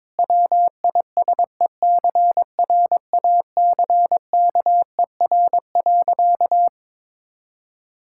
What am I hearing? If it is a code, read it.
Morse code, heard as a keyed tone, about 22 wpm, 690 Hz. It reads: WISECRACKER.
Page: morse-code-ninja-practice-22wpm